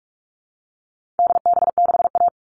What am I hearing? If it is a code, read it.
Morse code, heard as a keyed tone, about 44 wpm, 700 Hz. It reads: DB6A
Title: All Cut Numbers - 4-Digit 45wpm